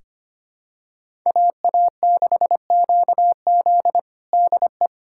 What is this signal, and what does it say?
Morse code, heard as a keyed tone, about 25 wpm, 700 Hz. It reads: AA6QZ DE
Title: USA Callsign DE Callsign - 5 Characters - 25wpm